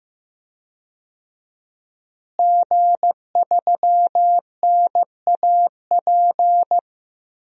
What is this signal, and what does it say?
Morse code, heard as a keyed tone, about 15 wpm, 700 Hz. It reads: G3NAP
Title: DXCC-100 Call Signs - 5 Characters - Repeated 3x - 15wpm